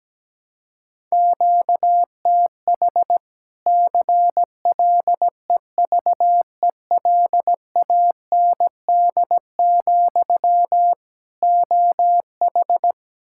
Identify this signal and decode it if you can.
Morse code, heard as a keyed tone, about 17 wpm, 700 Hz. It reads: QTH CLEVELAND, OH
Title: Practice QSO - Exchange 1 - Beginning - 17wpm